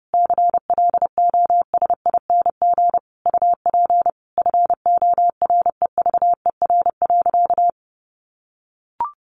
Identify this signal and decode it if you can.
Morse code, heard as a keyed tone, about 30 wpm, 700 Hz. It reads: CLOSING UP FOREVER.
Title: A Sound Of Thunder 30wpm